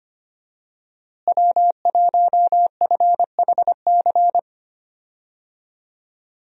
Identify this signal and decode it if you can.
Morse code, heard as a keyed tone, about 25 wpm, 700 Hz. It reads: W1FHC
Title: USA Call Signs - 5 Characters - Repeated 3x - 25wpm